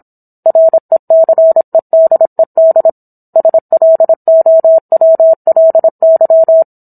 Morse code, heard as a keyed tone, about 26 wpm, 650 Hz. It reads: RECEDED SLOWLY